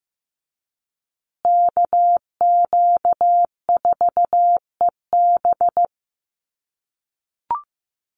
Morse code, heard as a keyed tone, about 15 wpm, 700 Hz. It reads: KQ4EB
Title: USA Call Signs Repeated 3x 15wpm